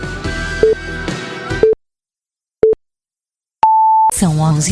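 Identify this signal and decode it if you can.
Radio time signal (time pips), with speech. It is heard from 0.6 to 4.1 s.
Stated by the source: Radio time signal from Orleans, France.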